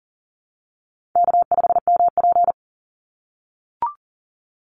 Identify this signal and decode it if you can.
Morse code, heard as a keyed tone, about 39 wpm, 700 Hz. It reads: K5MP
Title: Easy Call Signs - 40wpm